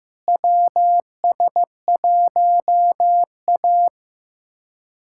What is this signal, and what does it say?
Morse code, heard as a keyed tone, about 15 wpm, 700 Hz. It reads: WS1A
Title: USA Callsign DE Callsign - 4 Characters - 15wpm